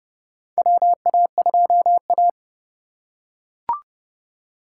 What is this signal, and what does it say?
Morse code, heard as a keyed tone, about 31 wpm, 700 Hz. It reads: WA2A
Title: USA Call Signs - 3x Character Spacing - 30wpm